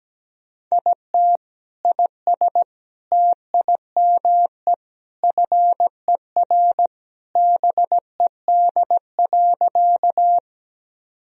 Morse code, heard as a keyed tone, about 17 wpm, 700 Hz. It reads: IT IS TIME FER BED.
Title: Sentences from Top 500 Words and Abbreviations 17wpm